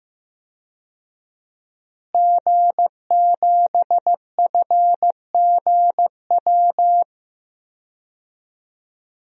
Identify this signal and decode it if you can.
Morse code, heard as a keyed tone, about 15 wpm, 700 Hz. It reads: G7FGW
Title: DXCC-100 Call Signs - 5 Characters - Repeated 2x - 15wpm